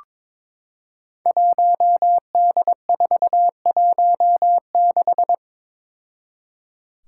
Morse code, heard as a keyed tone, about 22 wpm, 700 Hz. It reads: 1D416
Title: Five Challenging Letter-Number Combinations 22wpm